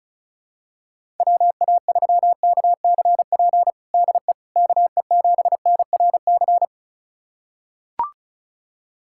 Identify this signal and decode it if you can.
Morse code, heard as a keyed tone, about 36 wpm, 700 Hz. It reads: WA3KCP DE KE7NRC
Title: USA Callsign DE Callsign - 6 Characters - 35wpm